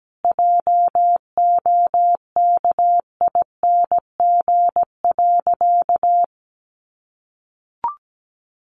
Morse code, heard as a keyed tone, about 17 wpm, 700 Hz. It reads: JOKING.
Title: A Sound Of Thunder 17wpm